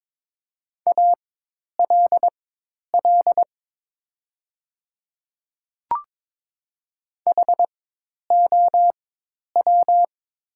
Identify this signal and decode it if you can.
Morse code, heard as a keyed tone, about 22 wpm, spaced as for 10 wpm, 700 Hz. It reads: ALL HOW
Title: 4x Character Spacing - Top 100 Words - 22wpm